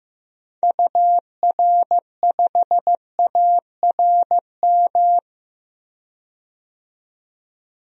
Morse code, heard as a keyed tone, about 15 wpm, 700 Hz. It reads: UR5ARM
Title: DXCC-100 Call Signs - 6 Characters - 15wpm